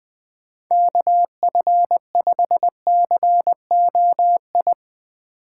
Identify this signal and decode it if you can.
Morse code, heard as a keyed tone, about 20 wpm, 700 Hz. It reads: KF5COI
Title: USA Call Signs - 6 Characters - 8x Character Spacing - 20wpm